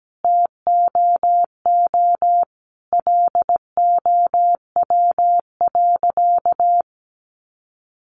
Morse code, heard as a keyed tone, about 17 wpm, 700 Hz. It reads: TOO LOW.
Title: Sentences from Top 1200 Words 17wpm